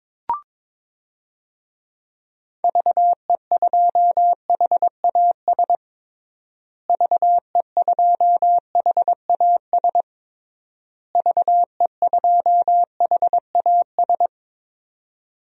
Morse code, heard as a keyed tone, about 22 wpm, 700 Hz. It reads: VE2HAS VE2HAS VE2HAS
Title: DXCC-100 Call Signs - 6 Characters - Repeated 3x - 22wpm